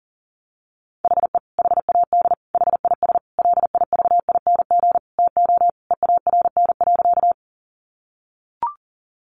Morse code, heard as a keyed tone, about 40 wpm, 700 Hz. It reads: HE HAD HIS LIVING TO EARN.
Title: Sentences from Top 1500 Words 40wpm